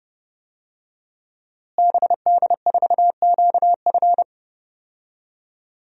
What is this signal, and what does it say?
Morse code, heard as a keyed tone, about 30 wpm, 700 Hz. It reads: BD4QF
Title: DXCC-100 Call Signs - 5 Characters - Repeated 3x - 30wpm